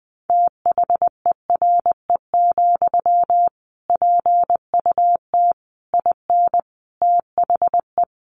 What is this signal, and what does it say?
Morse code, heard as a keyed tone, about 20 wpm, 700 Hz. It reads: THERE, PUT IN THE
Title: Sentences from Top 1200 Words 20wpm